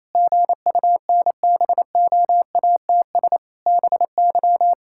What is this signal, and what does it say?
Morse code, heard as a keyed tone, about 28 wpm, 700 Hz. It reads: GUNBOATS BY
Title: Pangram Challenge v2 - 28wpm